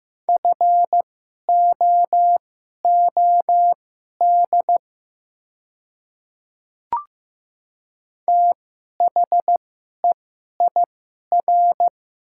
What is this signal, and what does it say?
Morse code, heard as a keyed tone, about 15 wpm, spaced as for 11 wpm, 700 Hz. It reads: FOOD THEIR
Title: 2x Character Spacing - Top 300 Words - 15wpm